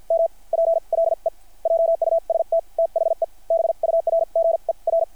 Morse code, heard as a keyed tone, about 50 wpm, spaced as for 40 wpm, 660 Hz. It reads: MPLE JUST THE BURGER
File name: archived / CRSnet